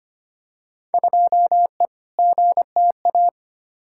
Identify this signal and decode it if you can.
Morse code, heard as a keyed tone, about 25 wpm, 700 Hz. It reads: 2E GTA
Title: Field Day Exchanges Encore - 25wpm